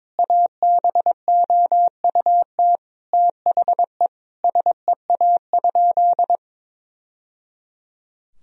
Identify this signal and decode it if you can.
Morse code, heard as a keyed tone, about 22 wpm, 700 Hz. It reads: ABOUT THE SEA?
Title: Sentences from Top 1000 Words - Review Part 1 - 22wpm